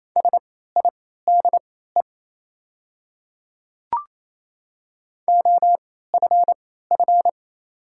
Morse code, heard as a keyed tone, about 27 wpm, spaced as for 16 wpm, 700 Hz. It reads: SIDE OFF
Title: 3x Character Spacing - Top 500 Words - 28wpm